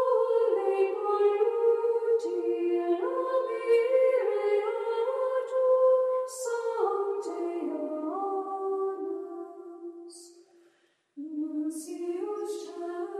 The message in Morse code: none